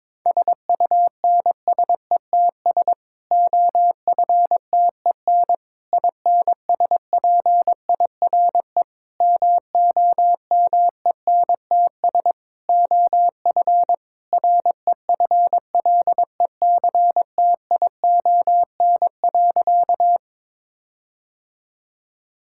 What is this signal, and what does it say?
Morse code, heard as a keyed tone, about 22 wpm, 700 Hz. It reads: SUNSETS OFTEN INSPIRE MOMENTS OF REFLECTION.
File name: Difficult Sentences with Hints - 22wpm